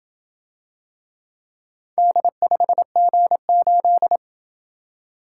Morse code, heard as a keyed tone, about 27 wpm, 700 Hz. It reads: D5G8